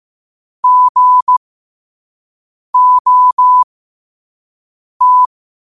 Morse code, heard as a keyed tone, about 15 wpm, spaced as for 5 wpm, 1000 Hz. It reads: GOT
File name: qotdfiles / 5wpm